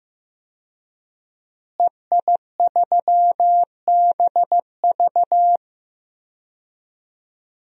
Morse code, heard as a keyed tone, about 15 wpm, 700 Hz. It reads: EI3BV
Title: DXCC-100 Call Signs - 5 Characters - Repeated 3x - 15wpm